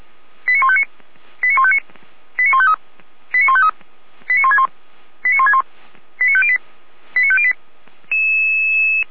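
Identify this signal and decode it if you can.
Five-tone selective call (ZVEI), heard as quick runs of five tones, tones of about 70 ms. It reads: 89159 89159 89153 89153 89161 89161 89508 89508
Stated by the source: ZVEI five-tone selective call 89160 received by SDR, logged by Feuerwehr Orlamünde.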